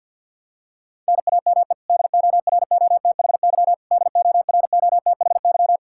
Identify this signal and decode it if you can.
Morse code, heard as a keyed tone, about 52 wpm, 700 Hz. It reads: NAME DOROTHY DOROTHY
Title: Practice QSO - Exchange 1 - Beginning - 50wpm